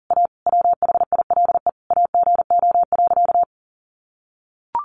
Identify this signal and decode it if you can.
Morse code, heard as a keyed tone, about 40 wpm, 700 Hz. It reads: A WHILE AGO.
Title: Sentences from Top 600 Words 40wpm